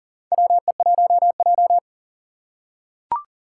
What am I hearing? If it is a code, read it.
Morse code, heard as a keyed tone, about 40 wpm, 700 Hz. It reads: WE1J
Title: USA Call Signs - 4 Characters - 3x Character Spacing - 40wpm